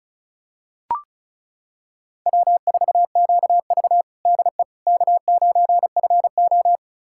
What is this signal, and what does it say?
Morse code, heard as a keyed tone, about 36 wpm, 700 Hz. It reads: W4QV DE K9FO
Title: USA Callsign DE Callsign - 4 Characters - 35wpm